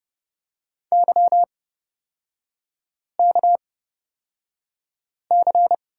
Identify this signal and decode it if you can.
Morse code, heard as a keyed tone, about 30 wpm, spaced as for 5 wpm, 700 Hz. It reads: YKC
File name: Lesson 38e - Letter q - Three Characters 30wpm